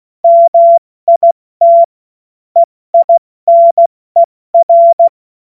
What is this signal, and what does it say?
Morse code, heard as a keyed tone, about 16 wpm, spaced as for 14 wpm, 680 Hz. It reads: MIT EINER
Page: cw-witze